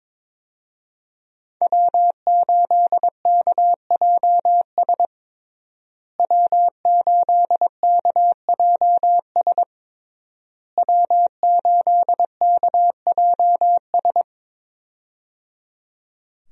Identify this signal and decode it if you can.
Morse code, heard as a keyed tone, about 21 wpm, 700 Hz. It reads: W8KJS W8KJS W8KJS
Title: USA Call Signs - 5 Characters - Repeated 3x - 22wpm